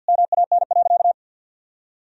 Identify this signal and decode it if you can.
Morse code, heard as a keyed tone, about 50 wpm, 700 Hz. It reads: MAN.
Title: Sentences from Top 1000 Words - Review Part 3 - 50wpm